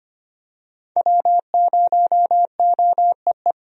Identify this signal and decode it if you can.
Morse code, heard as a keyed tone, about 25 wpm, 700 Hz. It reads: W0OEE
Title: USA Call Signs - 5 Characters - Repeated 3x - 25wpm